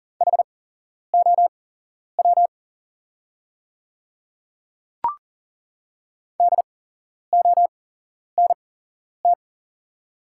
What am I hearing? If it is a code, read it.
Morse code, heard as a keyed tone, about 40 wpm, spaced as for 11 wpm, 690 Hz. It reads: HOW DONT
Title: 8x Character Spacing - Top 300 Words - 40wpm